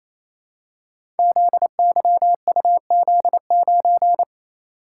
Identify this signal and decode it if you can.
Morse code, heard as a keyed tone, about 28 wpm, 700 Hz. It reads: ZYUZ9
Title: Code Groups - 5 Letter:Numbers - 28wpm